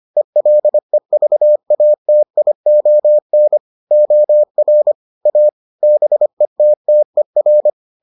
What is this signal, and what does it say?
Morse code, heard as a keyed tone, about 25 wpm, 600 Hz. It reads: ELEVATION OR A BETTER